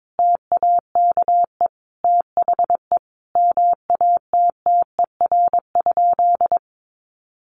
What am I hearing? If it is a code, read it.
Morse code, heard as a keyed tone, about 22 wpm, 700 Hz. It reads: TAKE THE MATTER?
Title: Sentences from Top 700 Words 22wpm